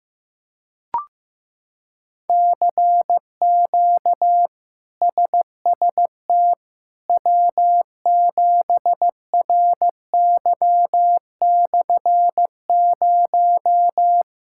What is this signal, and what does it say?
Morse code, heard as a keyed tone, about 16 wpm, 700 Hz. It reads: CQ SST W7RY/0
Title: K1USN SST QSO Practice - 15wpm